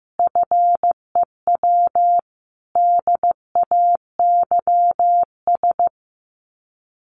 Morse code, heard as a keyed tone, about 15 wpm, 700 Hz. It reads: FEW DAYS
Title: Sets of 5 Words Encore2 15wpm